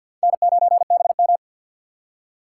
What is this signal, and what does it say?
Morse code, heard as a keyed tone, about 51 wpm, 700 Hz. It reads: N9BM
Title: USA Call Signs - 4x Character Spacing - 50wpm